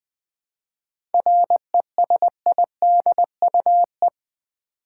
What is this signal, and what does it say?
Morse code, heard as a keyed tone, about 20 wpm, 700 Hz. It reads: RESIDUE
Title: 7 Letter Words - First 3 Letters Given - 20wpm